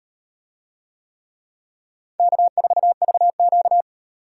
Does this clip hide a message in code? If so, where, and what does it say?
Yes: Morse code from 2.2 to 3.8 s, heard as a keyed tone, about 36 wpm, 700 Hz. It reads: K4VQ